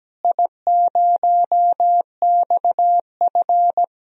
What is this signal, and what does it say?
Morse code, heard as a keyed tone, about 17 wpm, 700 Hz. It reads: I0XF
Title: DXCC-100 Call Signs - 4 Characters - Repeated 2x - 17wpm